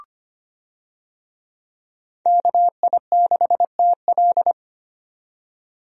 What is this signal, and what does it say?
Morse code, heard as a keyed tone, about 25 wpm, 700 Hz. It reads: KI6TL
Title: USA Call Signs - 5 Characters - 25wpm